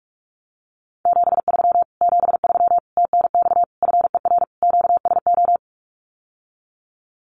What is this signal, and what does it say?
Morse code, heard as a keyed tone, about 44 wpm, 700 Hz. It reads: 73 73 TNX FER QSO
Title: Common QSO Phrases Encore 45wpm